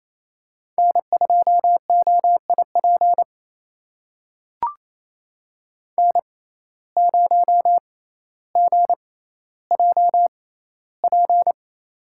Morse code, heard as a keyed tone, about 28 wpm, spaced as for 14 wpm, 700 Hz. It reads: N2OIP N0GJP
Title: USA Call Signs - 5 Characters - 6x Character Spacing - 28wpm